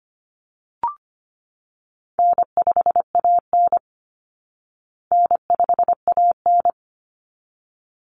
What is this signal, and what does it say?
Morse code, heard as a keyed tone, about 25 wpm, 700 Hz. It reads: N5AN N5AN
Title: Call Signs Repeated 3x - 25wpm